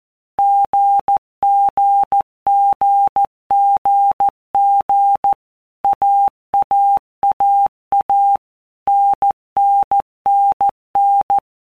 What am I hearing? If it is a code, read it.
Morse code, heard as a keyed tone, about 14 wpm, 800 Hz. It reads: GGGGG AAAA NNNN
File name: Basic Letter Groups By Blocks / Groups@13 / AWUNDG@13